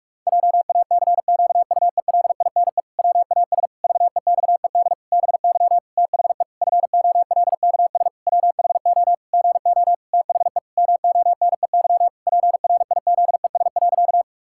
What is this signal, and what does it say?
Morse code, heard as a keyed tone, about 46 wpm, 700 Hz. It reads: JACQUELINE WAS VEXED BY THE FOLKS WHO GO THE MONEY PRIZES.
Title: Pangram Challenge v2 - 45wpm